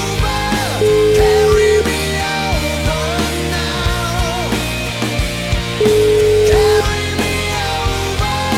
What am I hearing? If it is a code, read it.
Ringback tone (Europe, 425 Hz), with noise.